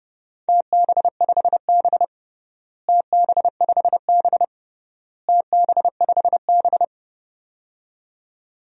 Morse code, heard as a keyed tone, about 30 wpm, 700 Hz. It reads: TB5B TB5B TB5B
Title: DXCC-100 Call Signs - 4 Characters - Repeated 3x - 30wpm